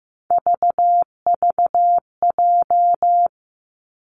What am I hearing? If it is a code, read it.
Morse code, heard as a keyed tone, about 15 wpm, 700 Hz. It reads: VVJ
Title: Three Challenging Letter-Number Combinations 15wpm